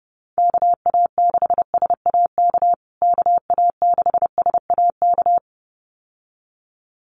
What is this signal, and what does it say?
Morse code, heard as a keyed tone, about 30 wpm, 700 Hz. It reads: KA6SAK KA6SAK
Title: USA Call Signs Repeated 2x 30wpm